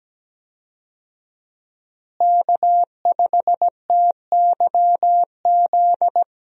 Morse code, heard as a keyed tone, about 17 wpm, 700 Hz. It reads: K5TYZ
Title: USA Call Signs - 5 Characters - Repeated 3x - 17wpm